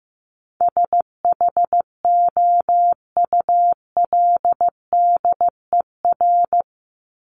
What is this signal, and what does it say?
Morse code, heard as a keyed tone, about 15 wpm, 700 Hz. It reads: SHOULDER
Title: Top 1501-1750 Words 15wpm